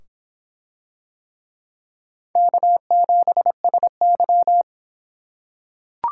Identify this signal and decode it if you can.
Morse code, heard as a keyed tone, about 25 wpm, 700 Hz. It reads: K7SY